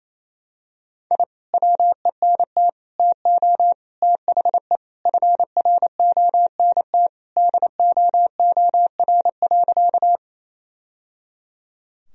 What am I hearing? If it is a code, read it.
Morse code, heard as a keyed tone, about 28 wpm, 700 Hz. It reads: I WENT TO THE FRONT DOOR.
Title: Sentences from Top 1000 Words - Review Part 2 - 28wpm